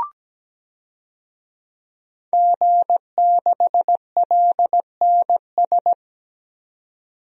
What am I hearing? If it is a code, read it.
Morse code, heard as a keyed tone, about 17 wpm, 700 Hz. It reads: G6LNS
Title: DXCC-100 Call Signs - 5 Characters - 17wpm